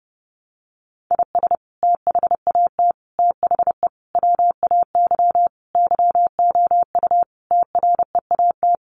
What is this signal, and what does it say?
Morse code, heard as a keyed tone, about 30 wpm, 700 Hz. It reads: IS THAT THE WAY YOU TREAT
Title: Sentences from Top 1300 Words 30wpm